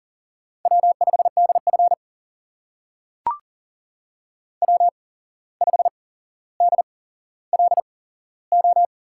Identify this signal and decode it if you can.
Morse code, heard as a keyed tone, about 40 wpm, spaced as for 16 wpm, 700 Hz. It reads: W5DF W5DLO
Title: USA Call Signs - 8x Character Spacing - 40wpm